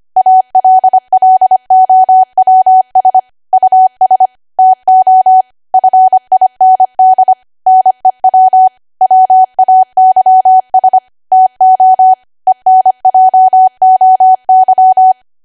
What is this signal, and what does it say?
Morse code, heard as a keyed tone, about 25 wpm, 750 Hz. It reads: ALLOWS US TO FIND NEW WAYS TO ENJOY
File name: Archive / 25 WPM / 25WPM